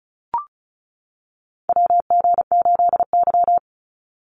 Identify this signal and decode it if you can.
Morse code, heard as a keyed tone, about 34 wpm, 700 Hz. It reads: WG8Y
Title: NA SOTA Chaser Callsigns 35wpm